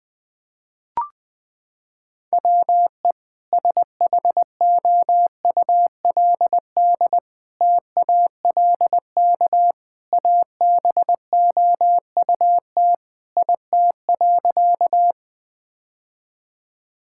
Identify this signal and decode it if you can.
Morse code, heard as a keyed tone, about 20 wpm, 700 Hz. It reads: WE SHOULD TALK ABOUT IT.
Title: Sentences with 2-Word Phrases - 20wpm